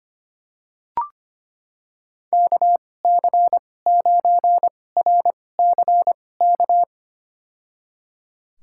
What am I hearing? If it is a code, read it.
Morse code, heard as a keyed tone, about 25 wpm, spaced as for 18 wpm, 700 Hz. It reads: KC9RCK
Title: USA Call Signs - 6 Characters - 2x Character Spacing - 25wpm